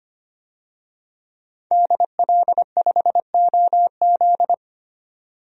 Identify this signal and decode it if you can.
Morse code, heard as a keyed tone, about 25 wpm, 700 Hz. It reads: DL5OZ
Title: DXCC-100 Call Signs - 5 Characters - 25wpm